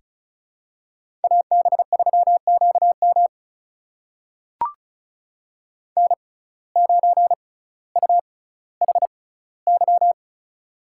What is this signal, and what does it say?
Morse code, heard as a keyed tone, about 36 wpm, spaced as for 18 wpm, 700 Hz. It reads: AB3QM N9UHY
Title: USA Call Signs - 5 Characters - 6x Character Spacing - 35wpm